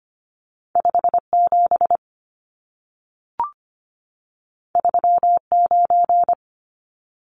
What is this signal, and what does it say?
Morse code, heard as a keyed tone, about 25 wpm, 700 Hz. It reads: 57 39
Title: Numbers - 2-Digit 25wpm